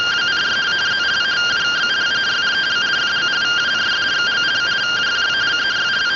Two-tone teleprinter signal, about 170 Hz shift.
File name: findings / CT 205 ax mips, iDose 3 0005.gen-18 rtty rx